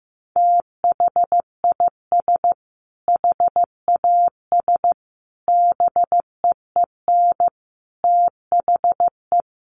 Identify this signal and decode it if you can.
Morse code, heard as a keyed tone, about 15 wpm, 700 Hz. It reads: THIS HAS BEEN THE
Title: Sentences from Top 1400 Words 15wpm